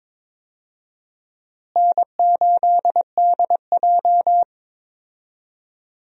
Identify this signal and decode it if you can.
Morse code, heard as a keyed tone, about 22 wpm, 700 Hz. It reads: N8DJ
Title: USA Call Signs - 4 Characters - Repeated 2x - 22wpm